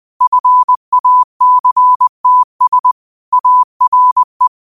Morse code, heard as a keyed tone, about 20 wpm, 1000 Hz. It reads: FACTS ARE